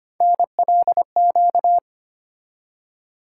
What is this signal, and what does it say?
Morse code, heard as a keyed tone, about 25 wpm, 700 Hz. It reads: NLQ
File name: Three Letter-Number Combinations - 25wpm